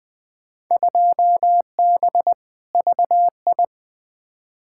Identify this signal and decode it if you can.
Morse code, heard as a keyed tone, about 20 wpm, 700 Hz. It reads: 2B VI
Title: Field Day Exchanges Encore - 20wpm